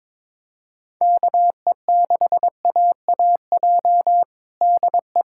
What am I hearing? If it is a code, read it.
Morse code, heard as a keyed tone, about 22 wpm, 700 Hz. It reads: KE6AAJ DE
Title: USA Callsign DE Callsign - 6 Characters - 22wpm